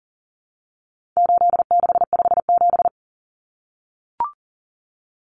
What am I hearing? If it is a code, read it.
Morse code, heard as a keyed tone, about 40 wpm, 700 Hz. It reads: 8657
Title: Common Cut Numbers - 4-Digit 40wpm